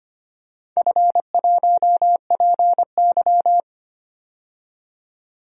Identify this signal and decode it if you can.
Morse code, heard as a keyed tone, about 25 wpm, 700 Hz. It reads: F1PY
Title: DXCC-100 Call Signs - 4 Characters - 25wpm